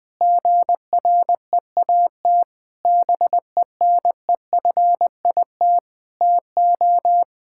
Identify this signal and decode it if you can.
Morse code, heard as a keyed tone, about 20 wpm, 700 Hz. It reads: GREAT BENEFIT TO
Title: Sentences from Top 1000 Words - Review Part 6 - 20wpm